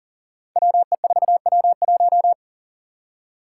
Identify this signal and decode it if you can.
Morse code, heard as a keyed tone, about 40 wpm, 700 Hz. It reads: WE4W1
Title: Five Challenging Letter-Number Combinations 40wpm